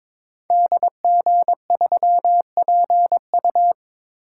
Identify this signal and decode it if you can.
Morse code, heard as a keyed tone, about 22 wpm, 700 Hz. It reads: DG3PU
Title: Code Groups - 5 Letter:Numbers - 22wpm